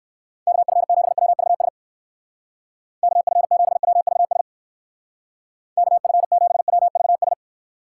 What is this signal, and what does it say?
Morse code, heard as a keyed tone, about 54 wpm, 700 Hz. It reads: KF7WVS KF7WVS KF7WVS
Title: Speed Racing - USA 2x3 Call Signs 45wpm